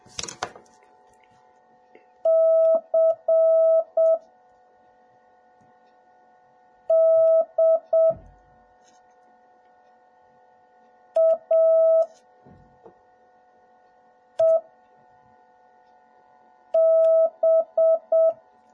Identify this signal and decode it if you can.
Morse code, heard as a keyed tone, about 7 wpm, spaced as for 3 wpm, 650 Hz. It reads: CDAEB